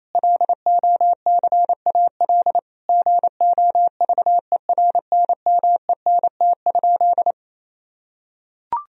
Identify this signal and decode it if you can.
Morse code, heard as a keyed tone, about 27 wpm, 700 Hz. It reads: LOCAL GOVERNMENT?
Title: Sentences from Top 1000 Words - Review Part 4 - 28wpm